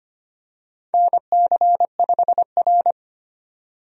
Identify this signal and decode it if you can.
Morse code, heard as a keyed tone, about 25 wpm, 700 Hz. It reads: NC5R
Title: USA Call Signs - 4 Characters - 25wpm